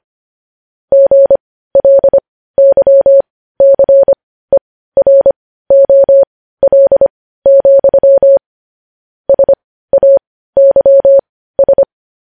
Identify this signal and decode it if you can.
Morse code, heard as a keyed tone, about 25 wpm, spaced as for 15 wpm, 550 Hz. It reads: GLYCEROL, SAYS
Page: news-headlines-in-morse-code-at-15-wpm